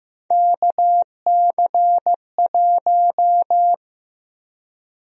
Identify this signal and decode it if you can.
Morse code, heard as a keyed tone, about 15 wpm, 700 Hz. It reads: KC1
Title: USA Call Sign Prefixes - 15wpm